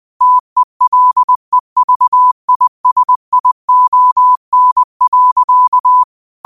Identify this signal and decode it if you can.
Morse code, heard as a keyed tone, about 20 wpm, 1000 Hz. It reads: TELEVISION.
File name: qotdfiles / 20wpm